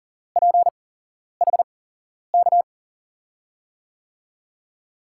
Morse code, heard as a keyed tone, about 40 wpm, spaced as for 11 wpm, 700 Hz. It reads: PHK
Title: USA Call Signs - 8x Character Spacing - 40wpm